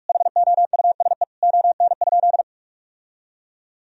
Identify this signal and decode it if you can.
Morse code, heard as a keyed tone, about 46 wpm, 700 Hz. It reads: HOUSE ON?
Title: Sentences from Top 1000 Words - Review Part 4 - 45wpm